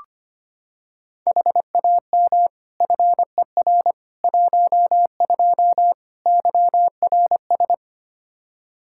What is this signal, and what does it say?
Morse code, heard as a keyed tone, about 25 wpm, 700 Hz. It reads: HAM FER 12 YRS
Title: Practice QSO - Exchange 3 - More Info - 25wpm